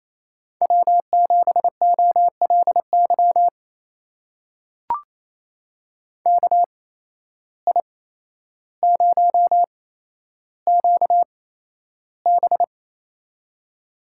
Morse code, heard as a keyed tone, about 28 wpm, spaced as for 12 wpm, 700 Hz. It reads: W7OLY KI0QB
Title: USA Call Signs - 8x Character Spacing - 28wpm